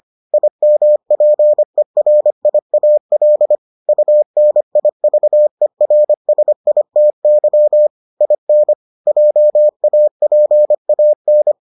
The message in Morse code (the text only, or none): IMPERIAL UNIVERSITY IN JAPAN